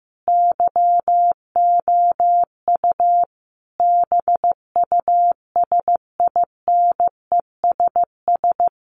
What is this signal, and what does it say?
Morse code, heard as a keyed tone, about 15 wpm, 700 Hz. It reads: YOU BUSINESS
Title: Sentences from Top 600 Words 15wpm